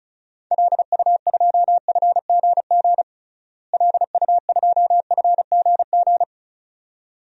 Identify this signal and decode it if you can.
Morse code, heard as a keyed tone, about 36 wpm, 700 Hz. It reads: LU2FGG LU2FGG
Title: DXCC-100 Call Signs - 6 Characters - Repeated 2x - 35wpm